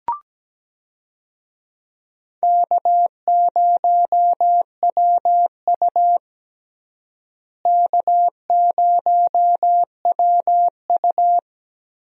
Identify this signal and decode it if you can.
Morse code, heard as a keyed tone, about 17 wpm, 700 Hz. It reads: K0WU K0WU
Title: USA Call Signs - 4 Characters - Repeated 2x - 17wpm